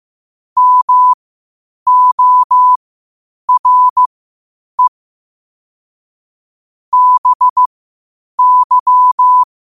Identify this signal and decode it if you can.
Morse code, heard as a keyed tone, about 15 wpm, spaced as for 8 wpm, 1000 Hz. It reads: MORE BY
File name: qotdfiles / 8wpm